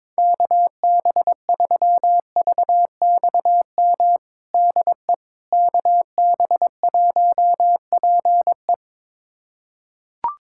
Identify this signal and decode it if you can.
Morse code, heard as a keyed tone, about 22 wpm, 700 Hz. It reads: KB3VXM DE KB1PE
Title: USA Callsign DE Callsign 22wpm